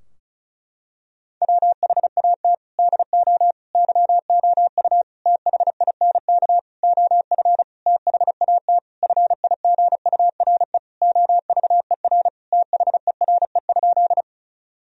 Morse code, heard as a keyed tone, about 36 wpm, 700 Hz. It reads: WHAT DO YOU THINK OF THAT FIGURE OVER THERE?
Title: Sentences from Top 1000 Words - Review Part 2 - 35wpm